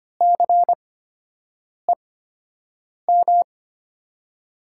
Morse code, heard as a keyed tone, about 25 wpm, spaced as for 7 wpm, 700 Hz. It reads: CEM
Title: USA Call Signs - 5 Characters - 8x Character Spacing - 25wpm